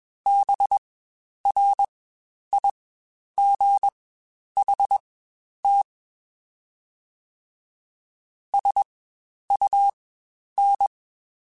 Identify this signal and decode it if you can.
Morse code, heard as a keyed tone, about 21 wpm, spaced as for 10 wpm, 800 Hz. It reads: BRIGHT SUN
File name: Spaced Characters / Spaced Characters-mp3s / Weather 5 12 sp @20